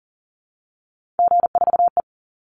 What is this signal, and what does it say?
Morse code, heard as a keyed tone, about 40 wpm, 700 Hz. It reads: G4E
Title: Call Signs - 40wpm